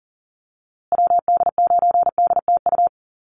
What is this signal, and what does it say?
Morse code, heard as a keyed tone, about 40 wpm, 700 Hz. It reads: WD9DTU
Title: USA Call Signs Repeated 3x 40wpm